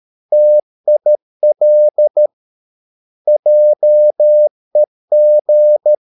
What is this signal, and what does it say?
Morse code, heard as a keyed tone, about 13 wpm, 600 Hz. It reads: TIL JEG
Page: Ferie 13 wpm